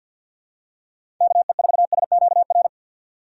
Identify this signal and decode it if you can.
Morse code, heard as a keyed tone, about 48 wpm, 700 Hz. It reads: KE4SQR
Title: USA Call Signs - 6 Characters - 6x Character Spacing - 50wpm